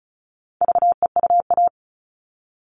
Morse code, heard as a keyed tone, about 34 wpm, 700 Hz. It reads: VEUA
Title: All Cut Numbers - 4-Digit 35wpm